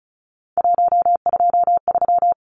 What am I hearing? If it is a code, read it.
Morse code, heard as a keyed tone, about 34 wpm, 700 Hz. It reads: 123
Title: Numbers - 3-Digit 35wpm